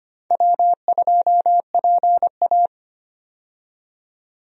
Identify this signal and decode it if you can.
Morse code, heard as a keyed tone, about 25 wpm, 700 Hz. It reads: W2PA
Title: USA Call Signs - 4 Characters - 25wpm